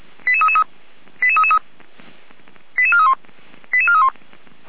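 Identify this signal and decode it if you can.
Five-tone selective call (ZVEI), heard as quick runs of five tones, tones of about 70 ms. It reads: 88333 88431 88431
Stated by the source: ZVEI five-tone selective call received by SDR, archived by the Orlamünde fire brigade (feuerwehr-orlamuende).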